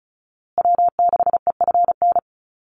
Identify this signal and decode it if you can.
Morse code, heard as a keyed tone, about 34 wpm, 700 Hz. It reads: W6EFN
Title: USA Call Signs Repeated 3x 35wpm